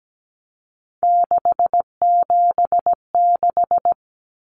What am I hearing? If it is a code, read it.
Morse code, heard as a keyed tone, about 17 wpm, 700 Hz. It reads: 676
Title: Numbers - 3-Digit 17wpm